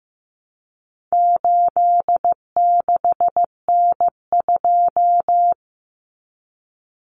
Morse code, heard as a keyed tone, about 15 wpm, 700 Hz. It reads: 86N2
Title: Common Cut Numbers - 4-Digit 15wpm